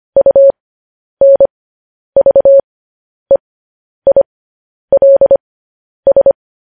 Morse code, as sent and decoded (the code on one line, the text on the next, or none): ..- -. ...- . .. .-.. ...
UNVEILS